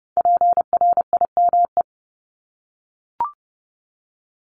Morse code, heard as a keyed tone, about 30 wpm, 700 Hz. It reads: PRIME
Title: Multi-letter Phonemes Repeated 3x 30wpm